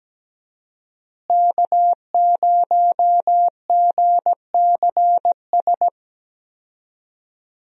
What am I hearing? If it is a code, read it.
Morse code, heard as a keyed tone, about 17 wpm, 700 Hz. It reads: K0GCS
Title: USA Call Signs - 5 Characters - Repeated 2x - 17wpm